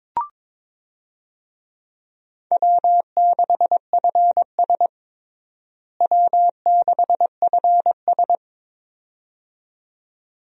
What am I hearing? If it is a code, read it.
Morse code, heard as a keyed tone, about 22 wpm, 700 Hz. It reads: W6FS W6FS
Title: USA Call Signs - 4 Characters - Repeated 2x - 22wpm